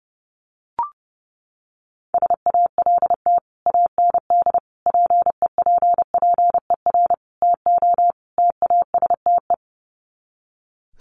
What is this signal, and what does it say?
Morse code, heard as a keyed tone, about 30 wpm, 700 Hz. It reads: SALT AND PEPPER TO TASTE
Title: Sets of 5 Words 30wpm